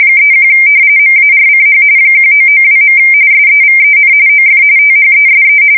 RTTY, about 45.45 baud. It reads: 31 MARCH 1992. SEVEN OF THESE
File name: fsk170 / battleships